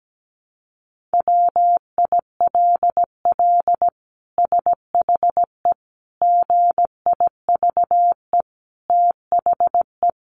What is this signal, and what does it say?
Morse code, heard as a keyed tone, about 17 wpm, 700 Hz. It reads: WILL SHE GIVE THE
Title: Sentences from Top 1400 Words 17wpm